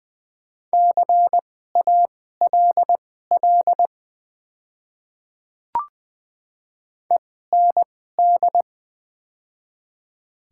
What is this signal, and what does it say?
Morse code, heard as a keyed tone, about 20 wpm, spaced as for 14 wpm, 700 Hz. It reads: CALL END
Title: 2x Character Spacing - Top 200 Words - 20wpm